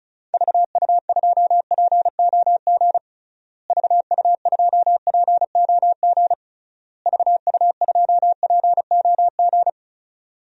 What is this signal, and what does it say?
Morse code, heard as a keyed tone, about 36 wpm, 700 Hz. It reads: VU2POG VU2POG VU2POG
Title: DXCC-100 Call Signs - 6 Characters - Repeated 3x - 35wpm